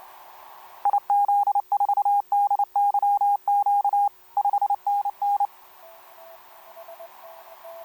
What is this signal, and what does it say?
Morse code, heard as a keyed tone, about 29 wpm, 810 Hz. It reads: IZ4DYQ 5NN
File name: kp2-ks9w sm1cq etc 12m cw 13mar2011 1339